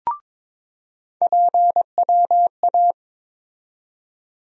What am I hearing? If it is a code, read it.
Morse code, heard as a keyed tone, about 22 wpm, 700 Hz. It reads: PWA
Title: USA Call Sign Postfixes - 3 Letters 22wpm